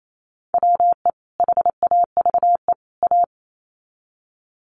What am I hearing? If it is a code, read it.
Morse code, heard as a keyed tone, about 28 wpm, 700 Hz. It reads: WE HAVE A
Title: Sets of 3 Words 28wpm